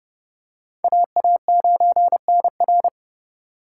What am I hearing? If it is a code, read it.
Morse code, heard as a keyed tone, about 29 wpm, 700 Hz. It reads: AA9NR
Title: USA Call Signs - 5 Characters - 4x Character Spacing - 30wpm